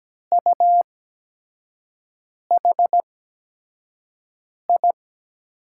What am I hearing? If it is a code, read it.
Morse code, heard as a keyed tone, about 17 wpm, spaced as for 5 wpm, 700 Hz. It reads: UHI
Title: USA Call Signs - 5 Characters - 8x Character Spacing - 17wpm